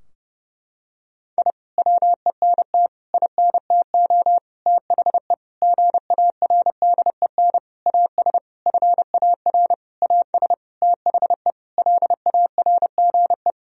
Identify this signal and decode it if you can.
Morse code, heard as a keyed tone, about 30 wpm, 700 Hz. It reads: I WENT INTO THE GARDEN AS FAR AS THE LARGE
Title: Sentences from Top 1000 Words - Review Part 4 - 30wpm